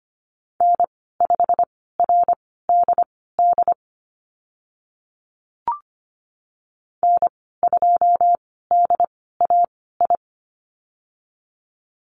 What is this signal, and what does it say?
Morse code, heard as a keyed tone, about 25 wpm, spaced as for 16 wpm, 700 Hz. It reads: N5RDD N2DAI
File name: Lesson 15g - Number 5 - Call Signs 25wpm